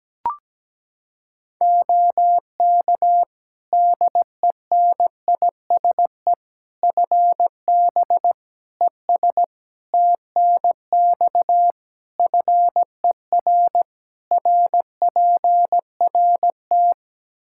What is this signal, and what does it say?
Morse code, heard as a keyed tone, about 17 wpm, 700 Hz. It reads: OK DENISE FB ES TNX FER RPRT
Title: Practice QSO - Exchange 2 - Info - 17wpm